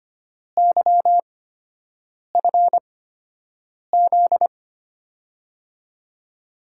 Morse code, heard as a keyed tone, about 25 wpm, spaced as for 7 wpm, 700 Hz. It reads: YFZ
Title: USA Call Signs - 8x Character Spacing - 25wpm